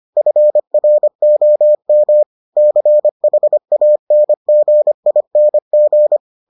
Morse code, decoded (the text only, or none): FROM CHANGING